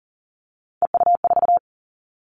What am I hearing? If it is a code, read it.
Morse code, heard as a keyed tone, about 40 wpm, 700 Hz. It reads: EU4
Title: All Cut Numbers - 3-Digit 40wpm